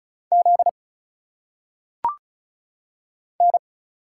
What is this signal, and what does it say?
Morse code, heard as a keyed tone, about 36 wpm, 700 Hz. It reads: Z N
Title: Single Letters - Phonetic Pronunciation 35wpm